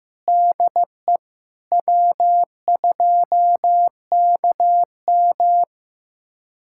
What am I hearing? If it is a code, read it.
Morse code, heard as a keyed tone, about 16 wpm, 700 Hz. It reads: DE W2KM
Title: USA Callsign DE Callsign - 4 Characters - 15wpm